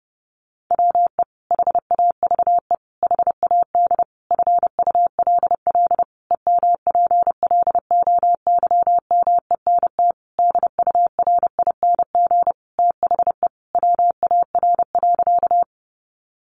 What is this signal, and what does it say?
Morse code, heard as a keyed tone, about 30 wpm, 700 Hz. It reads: WE HAVE HAD FULL EMPLOYMENT DURING THE WAR.
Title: Sentences from Top 1500 Words 30wpm